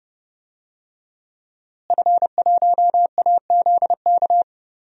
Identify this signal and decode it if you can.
Morse code, heard as a keyed tone, about 31 wpm, 700 Hz. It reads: F1AZK
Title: DXCC-100 Call Signs - 5 Characters - Repeated 2x - 30wpm